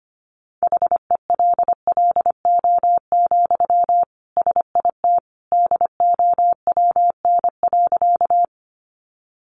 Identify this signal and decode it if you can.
Morse code, heard as a keyed tone, about 25 wpm, 700 Hz. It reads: HELLO, SIT DOWN.
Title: Sentences from Top 1100 Words 25wpm